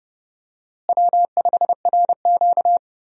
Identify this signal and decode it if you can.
Morse code, heard as a keyed tone, about 30 wpm, 700 Hz. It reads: W5RQ
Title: USA Call Signs - 4 Characters - Repeated 2x - 30wpm